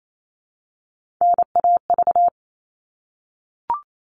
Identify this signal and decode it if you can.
Morse code, heard as a keyed tone, about 28 wpm, 700 Hz. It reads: NAV
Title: All Cut Numbers - 3-Digit 28wpm